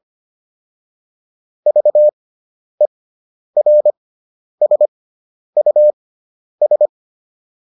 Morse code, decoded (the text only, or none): VERSUS